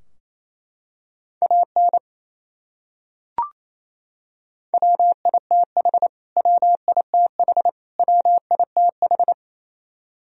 Morse code, heard as a keyed tone, about 28 wpm, 700 Hz. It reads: AN WITH WITH WITH
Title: Top 100 Words Repeated 28wpm